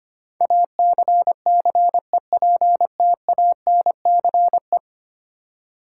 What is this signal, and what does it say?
Morse code, heard as a keyed tone, about 25 wpm, 700 Hz. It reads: ACCEPTANCE
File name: 10 Letter Words - First 3 Letters Given - 25wpm